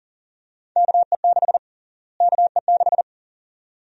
Morse code, heard as a keyed tone, about 40 wpm, 700 Hz. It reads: KE6 KE6
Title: USA Call Sign Prefixes - Repeated 2x - 40wpm